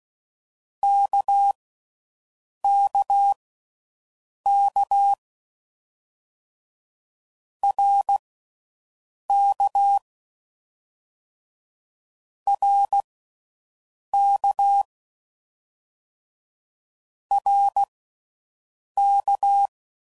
Morse code, heard as a keyed tone, about 16 wpm, spaced as for 6 wpm, 800 Hz. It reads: KKK RK RK RK